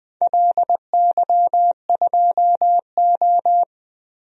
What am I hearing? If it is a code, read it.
Morse code, heard as a keyed tone, about 20 wpm, 700 Hz. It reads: LY2O
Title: DXCC-100 Call Signs - 4 Characters - 20wpm